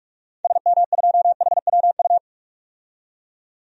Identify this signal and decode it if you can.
Morse code, heard as a keyed tone, about 46 wpm, 700 Hz. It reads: SM2HWU
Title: DXCC-100 Call Signs - 6 Characters - 45wpm